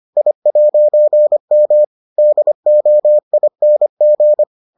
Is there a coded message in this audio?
Yes: Morse code from 0.2 to 4.4 s, heard as a keyed tone, about 25 wpm, 600 Hz. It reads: I'M DOING